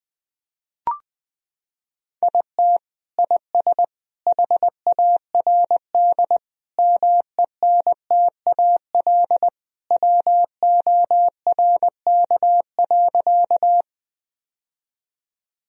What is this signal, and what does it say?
Morse code, heard as a keyed tone, about 21 wpm, 700 Hz. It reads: IT IS HARD MENTAL WORK.
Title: Sentences from Top 1000 Words - Review Part 4 - 20wpm